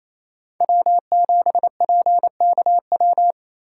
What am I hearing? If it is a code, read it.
Morse code, heard as a keyed tone, about 35 wpm, 700 Hz. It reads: W7PKW
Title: Speed Racing - USA 1x3 Call Signs 28wpm